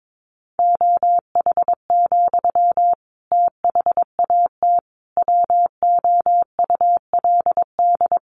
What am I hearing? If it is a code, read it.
Morse code, heard as a keyed tone, about 22 wpm, 700 Hz. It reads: OH, THAT WOULD
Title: Sentences from Top 1100 Words 22wpm